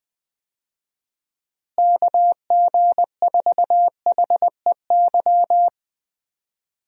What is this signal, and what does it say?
Morse code, heard as a keyed tone, about 20 wpm, 700 Hz. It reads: KG4HEY
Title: USA Call Signs - 6 Characters - Repeated 3x - 20wpm